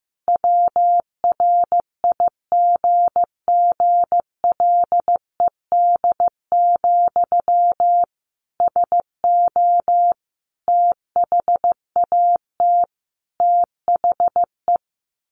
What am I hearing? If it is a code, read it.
Morse code, heard as a keyed tone, about 15 wpm, 700 Hz. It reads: WRIGGLED, SO THAT THE
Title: A Sound Of Thunder 15wpm